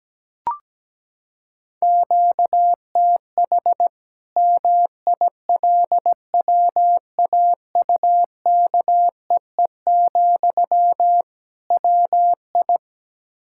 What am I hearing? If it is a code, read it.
Morse code, heard as a keyed tone, about 17 wpm, 700 Hz. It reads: QTH MILWAUKEE, WI
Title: Practice QSO - Exchange 1 - Beginning - 17wpm